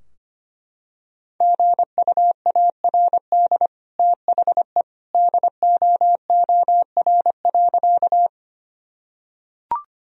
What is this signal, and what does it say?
Morse code, heard as a keyed tone, about 25 wpm, 700 Hz. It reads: GUARD THE DOOR.
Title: Sentences from Top 1000 Words - Review Part 4 - 25wpm